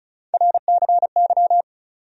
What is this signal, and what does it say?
Morse code, heard as a keyed tone, about 36 wpm, 700 Hz. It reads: RCY
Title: USA Call Sign Postfixes - 3 Letters 35wpm